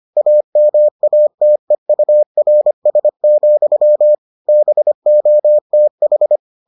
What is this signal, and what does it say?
Morse code, heard as a keyed tone, about 25 wpm, 600 Hz. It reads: AMATEURS, BOTH